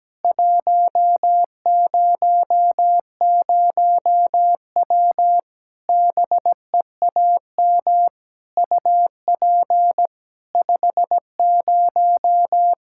Morse code, heard as a keyed tone, about 17 wpm, 700 Hz. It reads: 100W BEAM UP 50
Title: Practice QSO - Exchange 2 - Info - 17wpm